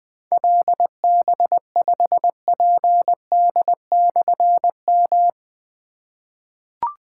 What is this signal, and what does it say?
Morse code, heard as a keyed tone, about 21 wpm, 700 Hz. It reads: LB5PD/M
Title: Difficult Call Signs 20wpm